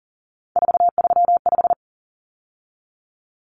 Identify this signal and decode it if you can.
Morse code, heard as a keyed tone, about 40 wpm, 700 Hz. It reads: 435
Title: Common Cut Numbers - 3-Digit 40wpm